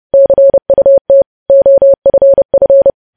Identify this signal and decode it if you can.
Morse code, heard as a keyed tone, about 30 wpm, 550 Hz. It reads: CUT OFF